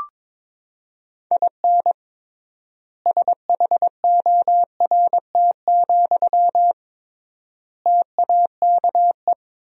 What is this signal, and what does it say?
Morse code, heard as a keyed tone, about 21 wpm, 700 Hz. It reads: IN SHORT, TAKE
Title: 3x Word-Spacing - Sentences from Top 400 Words - 22wpm